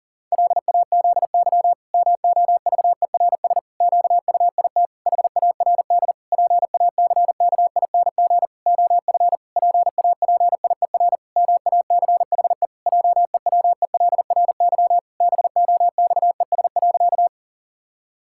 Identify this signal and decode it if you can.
Morse code, heard as a keyed tone, about 40 wpm, 700 Hz. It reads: LAZY MOVERS QUIT HARD PACKING OF PAPIER MACHE JEWELRY BOXES.
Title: Pangram Challenge v2 - 40wpm